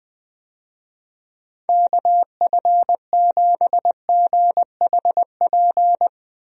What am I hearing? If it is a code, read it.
Morse code, heard as a keyed tone, about 20 wpm, 700 Hz. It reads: KF7GHP
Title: USA Call Signs - 6 Characters - Repeated 2x - 20wpm